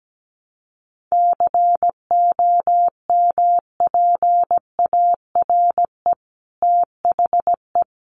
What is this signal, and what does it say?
Morse code, heard as a keyed tone, about 17 wpm, 700 Hz. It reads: COMPARE THE
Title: Sentences from Top 1300 Words 17wpm